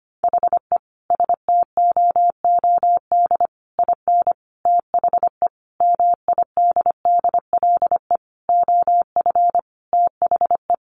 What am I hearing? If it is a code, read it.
Morse code, heard as a keyed tone, about 25 wpm, 700 Hz. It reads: HE STOOD IN THE MIDDLE OF THE
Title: Sentences from Top 1400 Words 25wpm